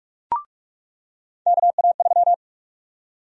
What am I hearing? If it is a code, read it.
Morse code, heard as a keyed tone, about 46 wpm, 700 Hz. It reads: KA3
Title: USA Call Sign Prefixes - 45wpm